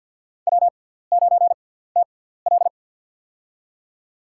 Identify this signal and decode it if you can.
Morse code, heard as a keyed tone, about 49 wpm, spaced as for 17 wpm, 700 Hz. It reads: W9TL
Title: USA Call Signs - 4 Characters - 6x Character Spacing - 50wpm